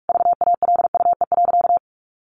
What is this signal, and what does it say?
Morse code, heard as a keyed tone, about 44 wpm, 700 Hz. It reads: VALUE.
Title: Sentences from Top 1400 Words 45wpm